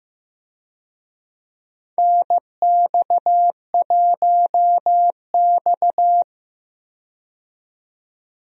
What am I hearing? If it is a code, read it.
Morse code, heard as a keyed tone, about 15 wpm, 700 Hz. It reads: NX1X
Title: USA Call Signs - 4 Characters - 15wpm